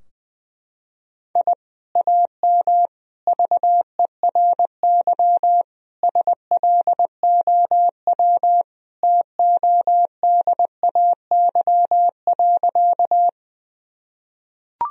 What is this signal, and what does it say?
Morse code, heard as a keyed tone, about 20 wpm, 700 Hz. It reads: I AM VERY SLOW TODAY.
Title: Sentences from Top 1000 Words - Review Part 2 - 20wpm